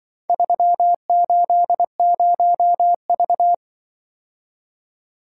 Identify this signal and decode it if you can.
Morse code, heard as a keyed tone, about 23 wpm, 700 Hz. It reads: 380V
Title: Speed-Racing - Four Letter-Number Combinations - 20wpm